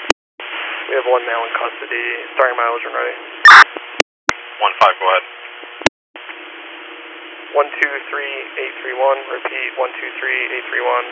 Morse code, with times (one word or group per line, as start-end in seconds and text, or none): none